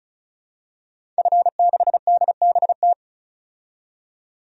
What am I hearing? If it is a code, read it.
Morse code, heard as a keyed tone, about 34 wpm, 700 Hz. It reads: F6DBT
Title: DXCC-100 Call Signs - 5 Characters - 35wpm